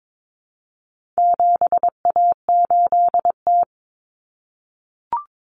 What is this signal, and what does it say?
Morse code, heard as a keyed tone, about 22 wpm, 700 Hz. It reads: 7A8T
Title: Common Cut Numbers - 4-Digit 22wpm